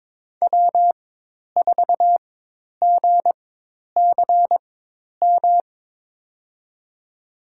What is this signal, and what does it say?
Morse code, heard as a keyed tone, about 22 wpm, spaced as for 10 wpm, 700 Hz. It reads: W4GCM
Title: USA Call Signs - 5 Characters - 4x Character Spacing - 22wpm